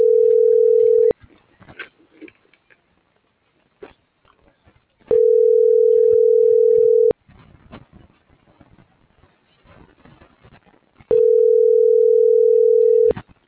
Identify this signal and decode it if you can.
Ringback tone (North America), 440 + 480 Hz.
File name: CENTRAL TERRESTRE / out vane.lazaro - .32321